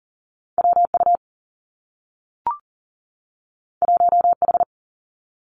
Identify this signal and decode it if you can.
Morse code, heard as a keyed tone, about 40 wpm, 700 Hz. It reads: WU 1H
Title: Two Challenging Letter-Number Combinations 40wpm